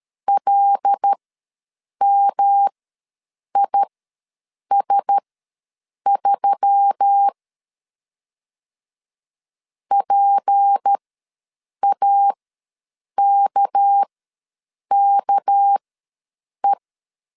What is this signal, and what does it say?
Morse code, heard as a keyed tone, about 13 wpm, spaced as for 7 wpm, 800 Hz. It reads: LMIS3 PAKKE
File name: LEZIONI / 6 PKX0 / 6 PK ws 12wpm 3116